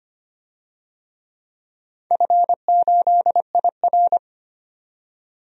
Morse code, heard as a keyed tone, about 25 wpm, 700 Hz. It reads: F8IR
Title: DXCC-100 Call Signs - 4 Characters - 25wpm